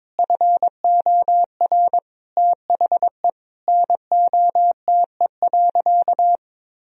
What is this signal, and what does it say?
Morse code, heard as a keyed tone, about 23 wpm, 700 Hz. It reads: FOR THE NOTE.
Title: Sentences from Top 1000 Words - Review Part 2 - 22wpm